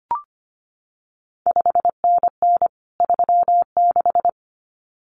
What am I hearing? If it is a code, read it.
Morse code, heard as a keyed tone, about 25 wpm, 700 Hz. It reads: 5NN 36
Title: CQ WW DX Contest Exchange Encore - 25wpm